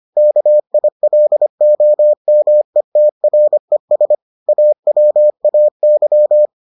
Morse code, heard as a keyed tone, about 25 wpm, 600 Hz. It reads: KILOMETRES AWAY